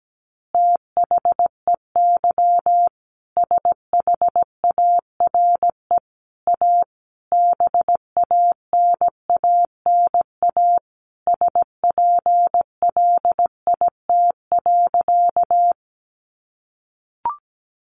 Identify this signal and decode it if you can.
Morse code, heard as a keyed tone, about 17 wpm, 700 Hz. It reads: THEY SHARE A BANANA SPLIT.
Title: Additional Easy Phrases 17wpm